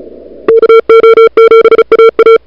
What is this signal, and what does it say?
Morse code, heard as a keyed tone, about 34 wpm, 440 Hz. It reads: KO7AA
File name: CWSprint / February 2007 CW Sprint / 01-2007 CW Sprint Hour1